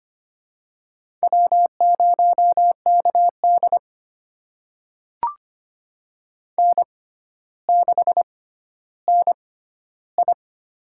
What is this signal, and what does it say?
Morse code, heard as a keyed tone, about 25 wpm, spaced as for 13 wpm, 700 Hz. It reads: W0KD N6NI
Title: USA Call Signs - 4 Characters - 6x Character Spacing - 25wpm